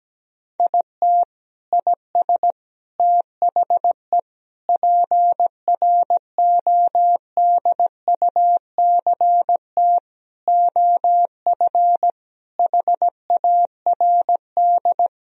Morse code, heard as a keyed tone, about 17 wpm, 700 Hz. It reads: IT IS THE PRODUCT OF HARD
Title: Sentences from Top 1000 Words - Review Part 2 - 17wpm